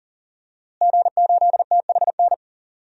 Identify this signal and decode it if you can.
Morse code, heard as a keyed tone, about 40 wpm, 700 Hz. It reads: G8THN
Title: DXCC-100 Call Signs - 5 Characters - Repeated 2x - 40wpm